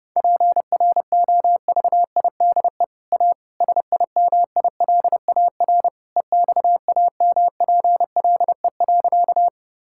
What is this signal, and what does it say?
Morse code, heard as a keyed tone, about 32 wpm, 700 Hz. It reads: PROVIDE A SIMILAR EXAMPLE.
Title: Sentences from Top 1000 Words - Review Part 5 - 30wpm